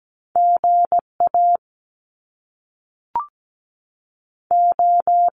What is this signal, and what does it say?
Morse code, heard as a keyed tone, about 17 wpm, 700 Hz. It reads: G A O
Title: Top Two Letter Combinations in Words 17wpm